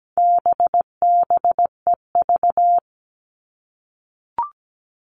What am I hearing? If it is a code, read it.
Morse code, heard as a keyed tone, about 17 wpm, 700 Hz. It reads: BBEV
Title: All Cut Numbers - 4-Digit 17wpm